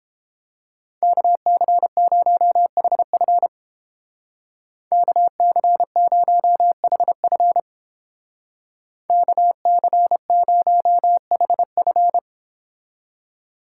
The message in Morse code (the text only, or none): KC0HF KC0HF KC0HF